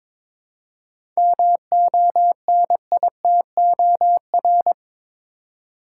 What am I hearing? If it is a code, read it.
Morse code, heard as a keyed tone, about 22 wpm, 700 Hz. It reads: MONITOR
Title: Seven Letter Words - 22wpm